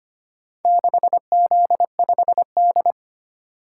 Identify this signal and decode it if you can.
Morse code, heard as a keyed tone, about 25 wpm, 700 Hz. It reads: 6Z5D
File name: practice / 25wpm / Four Letter-Number Combinations - 25wpm